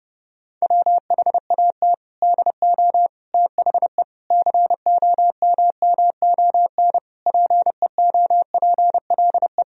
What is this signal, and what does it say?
Morse code, heard as a keyed tone, about 30 wpm, 700 Hz. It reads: WHAT DO THE COMMON PEOPLE
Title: Sentences from Top 1000 Words - Review Part 3 - 30wpm